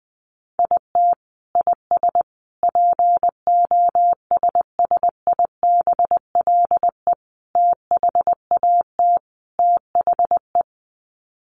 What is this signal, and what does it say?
Morse code, heard as a keyed tone, about 20 wpm, 700 Hz. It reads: IT IS POSSIBLE THAT THE
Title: Sets of 5 Words Encore2 20wpm